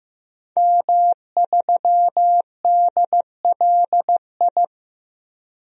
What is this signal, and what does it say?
Morse code, heard as a keyed tone, about 15 wpm, 700 Hz. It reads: M3DLI
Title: DXCC-100 Call Signs - 5 Characters - 15wpm